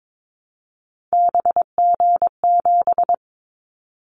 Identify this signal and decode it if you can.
Morse code, heard as a keyed tone, about 22 wpm, 700 Hz. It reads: BG7
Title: Callsign Prefixes for Top 100 DXCC - 22wpm